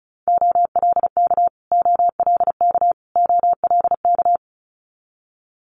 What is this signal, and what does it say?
Morse code, heard as a keyed tone, about 36 wpm, 700 Hz. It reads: OLK OLK OLK
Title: Multi-letter Phonemes Repeated 3x 35wpm